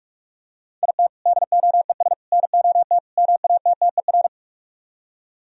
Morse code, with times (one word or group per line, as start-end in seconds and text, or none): 0.8-1.1 IT
1.3-2.1 DOES
2.3-3.0 NOT
3.2-4.3 MATTER